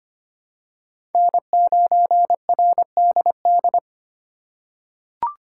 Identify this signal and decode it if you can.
Morse code, heard as a keyed tone, about 25 wpm, 700 Hz. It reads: N9RDD
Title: USA Call Signs - 5 Characters - 6x Character Spacing - 25wpm